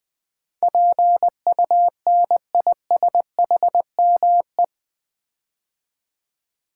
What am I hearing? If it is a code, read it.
Morse code, heard as a keyed tone, about 20 wpm, 700 Hz. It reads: PUNISHME
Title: Build A Word - 4 Letters At A Time - 10-Letter Words - 20wpm